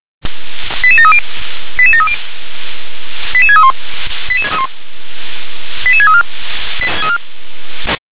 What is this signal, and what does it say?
Five-tone selective call (ZVEI), heard as quick runs of five tones, tones of about 70 ms. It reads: 88730 88730 88631 88634